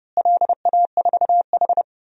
Morse code, heard as a keyed tone, about 30 wpm, 700 Hz. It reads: LA4H
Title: DXCC-100 Call Signs - 4 Characters - Repeated 2x - 30wpm